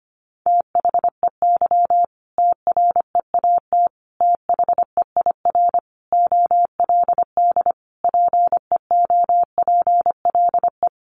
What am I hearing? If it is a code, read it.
Morse code, heard as a keyed tone, about 25 wpm, 700 Hz. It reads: THEY TREAT THEIR OLD PEOPLE
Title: Sentences from Top 1300 Words 25wpm